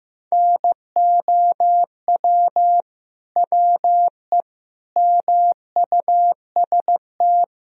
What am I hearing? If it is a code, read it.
Morse code, heard as a keyed tone, about 16 wpm, 700 Hz. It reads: NOW WE MUST
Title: Sentences from Top 1000 Words - Review Part 2 - 15wpm